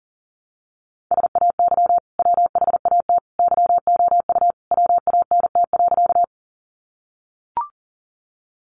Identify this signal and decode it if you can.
Morse code, heard as a keyed tone, about 40 wpm, 700 Hz. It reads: SAY WHAT YOU WANT.
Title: Sentences from Top 200 Words 40wpm